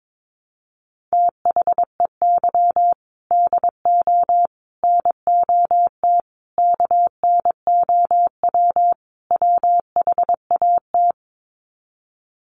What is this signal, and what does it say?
Morse code, heard as a keyed tone, about 22 wpm, 700 Hz. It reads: THEY DO NOT KNOW WHAT
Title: Sets of 5 Words Encore 22wpm